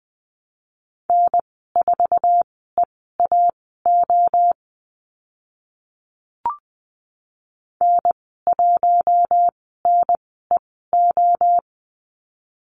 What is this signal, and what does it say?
Morse code, heard as a keyed tone, about 20 wpm, spaced as for 14 wpm, 700 Hz. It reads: N4EAO N1NEO
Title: 062 Lesson 11g - Letter h - Call Signs 20wpm